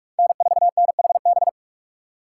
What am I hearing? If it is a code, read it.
Morse code, heard as a keyed tone, about 42 wpm, 700 Hz. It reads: N4NHB
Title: USA Call Signs - 5 Characters - 8x Character Spacing - 45wpm